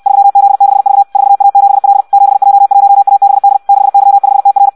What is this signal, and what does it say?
Morse code, heard as a keyed tone, about 64 wpm, 800 Hz. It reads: 3Z6V 4A7L 7J1ABD 6Y5TR
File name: Audio practica CW / qrq-tx-66 2